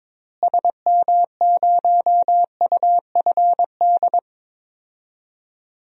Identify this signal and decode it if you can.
Morse code, heard as a keyed tone, about 22 wpm, 700 Hz. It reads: SM0UFD
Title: DXCC-100 Call Signs - 6 Characters - Repeated 3x - 22wpm